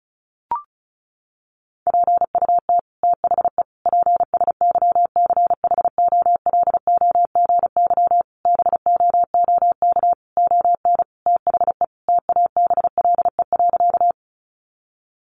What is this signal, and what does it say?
Morse code, heard as a keyed tone, about 34 wpm, 700 Hz. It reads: PUT THE PSYCHOLOGY BOOK ON THE TABLE.
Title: Sentences from Top 1400 Words 35wpm